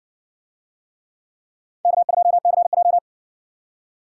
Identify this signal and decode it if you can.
Morse code, heard as a keyed tone, about 60 wpm, 700 Hz. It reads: K2CJ